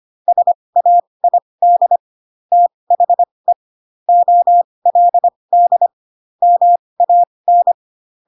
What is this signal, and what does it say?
Morse code, heard as a keyed tone, about 25 wpm, spaced as for 20 wpm, 700 Hz. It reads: SAID THE OLD MAN
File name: Pinocchio / Pinocchio ch8 15 25 ik1hge